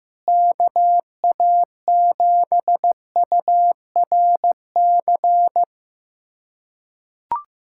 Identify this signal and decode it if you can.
Morse code, heard as a keyed tone, about 15 wpm, 700 Hz. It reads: KA7URC
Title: USA Callsign DE Callsign - 6 Characters - 15wpm